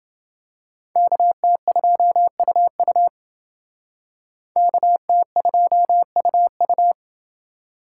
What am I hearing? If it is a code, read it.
Morse code, heard as a keyed tone, about 28 wpm, 700 Hz. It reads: KT2UU KT2UU